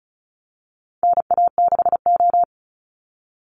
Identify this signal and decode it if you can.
Morse code, heard as a keyed tone, about 34 wpm, 700 Hz. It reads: NA6O
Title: Call Signs - 35wpm